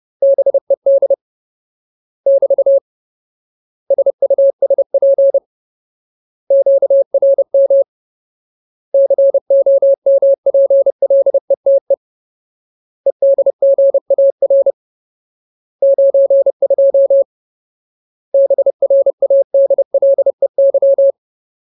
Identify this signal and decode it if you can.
Morse code, heard as a keyed tone, about 30 wpm, 550 Hz. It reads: BED <BT> SUSP QRM COMPLETE EDGAR 92 BRADLEY
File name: Ultimate Word List / 30WPM Ultimate